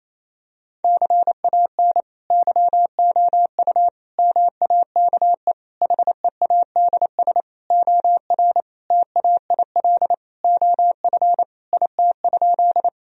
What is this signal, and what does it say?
Morse code, heard as a keyed tone, about 28 wpm, 700 Hz. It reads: CAN YOU MAKE HEADS OR TAIL OF IT?
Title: Sentences from Top 1000 Words - Review Part 3 - 28wpm